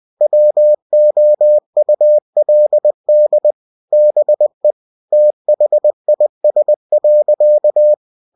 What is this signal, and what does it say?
Morse code, heard as a keyed tone, about 20 wpm, 600 Hz. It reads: WOULD BE THIS.
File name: cw-books / winnie-the-pooh / winnie-20wpm-page-0061